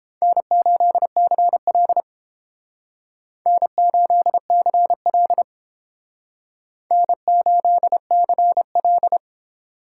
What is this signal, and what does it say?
Morse code, heard as a keyed tone, about 30 wpm, 700 Hz. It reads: N8CL N8CL N8CL